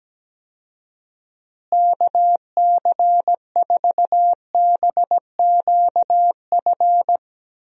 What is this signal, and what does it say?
Morse code, heard as a keyed tone, about 17 wpm, 700 Hz. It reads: KC4BQF
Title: USA Call Signs - 6 Characters - Repeated 3x - 17wpm